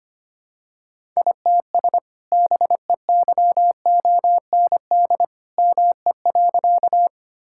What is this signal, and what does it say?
Morse code, heard as a keyed tone, about 25 wpm, 700 Hz. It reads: ITS BEYOND ME.
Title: Sentences from Top 1000 Words - Review Part 6 - 25wpm